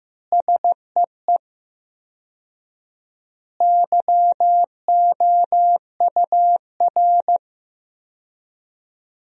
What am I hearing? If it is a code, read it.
Morse code, heard as a keyed tone, about 16 wpm, 700 Hz. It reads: SEE YOUR
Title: 4x Word-Spacing - Sentences from Top 500 Words - 15wpm